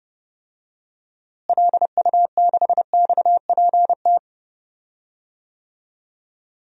Morse code, heard as a keyed tone, about 32 wpm, 700 Hz. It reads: LU6XPT
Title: DXCC-100 Call Signs - 6 Characters - Repeated 2x - 30wpm